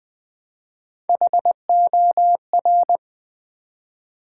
Morse code, heard as a keyed tone, about 20 wpm, 700 Hz. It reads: HOR
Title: Build A Word - 1 Letter At A Time - 4-Letter Words - 20wpm